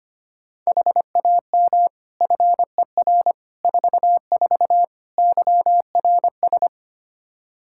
Morse code, heard as a keyed tone, about 25 wpm, 700 Hz. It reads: HAM FER 44 YRS
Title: Practice QSO - Exchange 3 - More Info - 25wpm